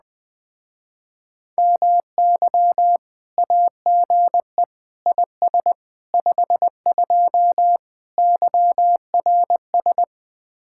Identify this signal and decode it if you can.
Morse code, heard as a keyed tone, about 20 wpm, 700 Hz. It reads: MY AGE IS 52 YRS